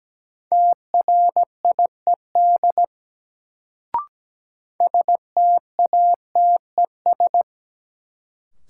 Morse code, heard as a keyed tone, about 17 wpm, 700 Hz. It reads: TRIED STATES
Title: Sets of 2 Words - New Word Drill - 17wpm